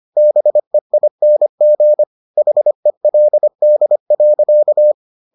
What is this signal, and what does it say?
Morse code, heard as a keyed tone, about 25 wpm, 600 Hz. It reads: BEING HELD.